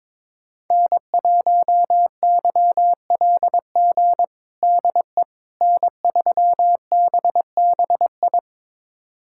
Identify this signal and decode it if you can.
Morse code, heard as a keyed tone, about 22 wpm, 700 Hz. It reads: N1YLG DE N3BBI
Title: USA Callsign DE Callsign - 5 Characters - 22wpm